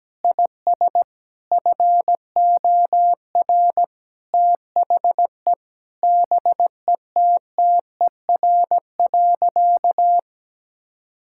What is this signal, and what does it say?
Morse code, heard as a keyed tone, about 17 wpm, 700 Hz. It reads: IS FOR THE BETTER.
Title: Sentences from Top 1000 Words - Review Part 5 - 17wpm